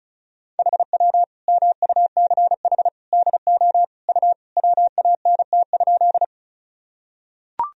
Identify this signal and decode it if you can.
Morse code, heard as a keyed tone, about 36 wpm, 700 Hz. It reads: HW MUCH DO U WANT?
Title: Sentences from Top 200 Words and Abbreviations 35wpm